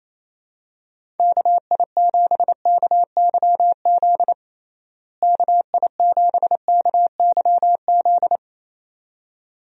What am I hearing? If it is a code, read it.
Morse code, heard as a keyed tone, about 28 wpm, 700 Hz. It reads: KI7KYZ KI7KYZ
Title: USA Call Signs - 6 Characters - Repeated 2x - 28wpm